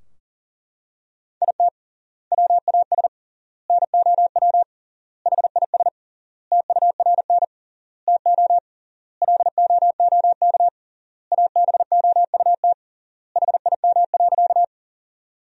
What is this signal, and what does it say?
Morse code, heard as a keyed tone, about 40 wpm, 700 Hz. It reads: IT WAS NOW HIS TURN TO LOOK ABOUT HIM.
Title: 3x Word-Spacing - Sentences from Top 200 Words - 40wpm